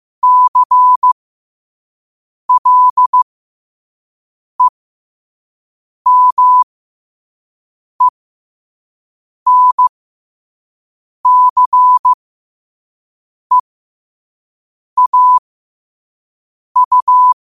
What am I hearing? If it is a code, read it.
Morse code, heard as a keyed tone, about 15 wpm, spaced as for 5 wpm, 1000 Hz. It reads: CLEMENCEAU
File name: qotdfiles / 5wpm